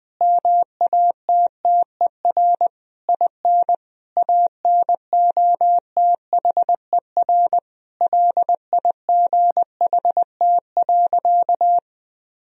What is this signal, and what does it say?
Morse code, heard as a keyed tone, about 20 wpm, 700 Hz. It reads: MATTER IN ANOTHER LIGHT.
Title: Sentences from Top 1000 Words - Review Part 1 - 20wpm